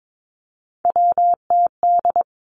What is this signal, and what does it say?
Morse code, heard as a keyed tone, about 22 wpm, 700 Hz. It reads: WTD
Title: Three Challenging Letter-Number Combinations 22wpm